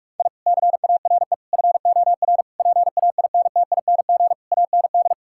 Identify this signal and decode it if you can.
Morse code, heard as a keyed tone, about 46 wpm, 700 Hz. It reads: I CARE FOR PAINTING AND
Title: Sentences from Top 1000 Words - Review Part 6 - 45wpm